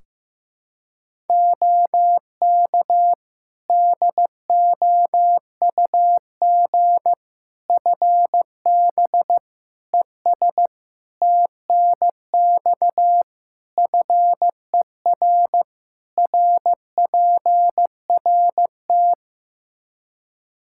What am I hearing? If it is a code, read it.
Morse code, heard as a keyed tone, about 15 wpm, 700 Hz. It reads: OK DOUG FB ES TNX FER RPRT
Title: K1USN SST QSO Practice 15wpm